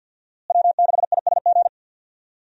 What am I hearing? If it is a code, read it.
Morse code, heard as a keyed tone, about 48 wpm, 700 Hz. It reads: W6ISG
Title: USA Call Signs - 5 Characters - Repeated 2x - 50wpm